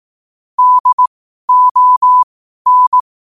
Morse code, heard as a keyed tone, about 18 wpm, spaced as for 12 wpm, 1000 Hz. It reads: DON
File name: qotdfiles / 12wpm